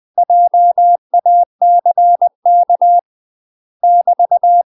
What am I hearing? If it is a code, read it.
Morse code, heard as a keyed tone, about 20 wpm, 700 Hz. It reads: JACK <BT>
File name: MASTER QSOs TRAINING/20WPM